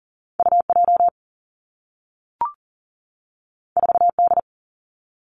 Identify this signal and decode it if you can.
Morse code, heard as a keyed tone, about 41 wpm, 700 Hz. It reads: UJ 4D
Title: Two Challenging Letter-Number Combinations 40wpm